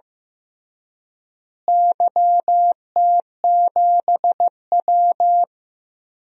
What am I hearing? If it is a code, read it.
Morse code, heard as a keyed tone, about 15 wpm, 700 Hz. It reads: YT7W
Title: DXCC-100 Call Signs - 4 Characters - Repeated 2x - 15wpm